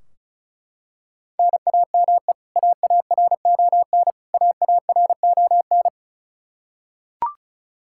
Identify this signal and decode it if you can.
Morse code, heard as a keyed tone, about 34 wpm, 700 Hz. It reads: NAME AARON AARON
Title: Practice QSO - Exchange 1 - Beginning - 35wpm